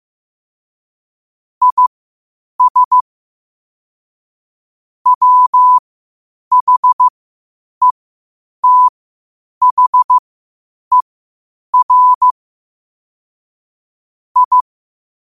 Morse code, heard as a keyed tone, about 15 wpm, spaced as for 8 wpm, 1000 Hz. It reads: IS WHETHER I
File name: qotdfiles / 8wpm